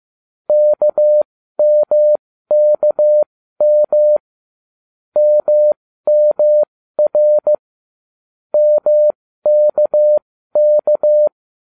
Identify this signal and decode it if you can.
Morse code, heard as a keyed tone, about 15 wpm, spaced as for 13 wpm, 600 Hz. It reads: KMKM MMR MKK